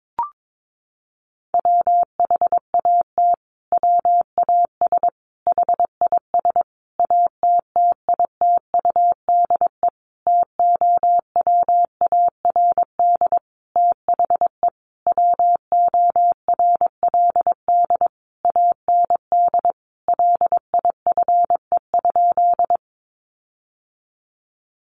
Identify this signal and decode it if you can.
Morse code, heard as a keyed tone, about 22 wpm, 700 Hz. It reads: WHAT WAS HIS ATTITUDE TOWARD THE WORLD AND LIFE?
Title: Sentences from Top 1300 Words 22wpm